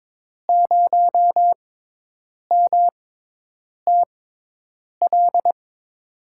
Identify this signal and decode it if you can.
Morse code, heard as a keyed tone, about 22 wpm, spaced as for 8 wpm, 700 Hz. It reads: 0MTL
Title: USA Call Signs - 6 Characters - 6x Character Spacing - 22wpm